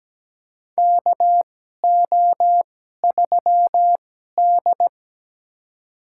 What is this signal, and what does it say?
Morse code, heard as a keyed tone, about 17 wpm, spaced as for 12 wpm, 700 Hz. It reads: KO3D
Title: USA Call Signs - 4 Characters - 2x Character Spacing - 17wpm